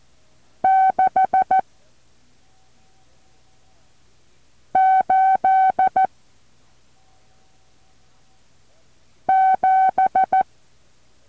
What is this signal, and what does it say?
Morse code, heard as a keyed tone, about 14 wpm, 760 Hz. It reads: 6 8 7
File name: intl morse code2